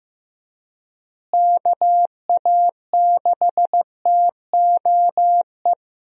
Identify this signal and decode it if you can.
Morse code, heard as a keyed tone, about 15 wpm, 700 Hz. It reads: KA6TOE
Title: USA Call Signs - 6 Characters - Repeated 2x - 15wpm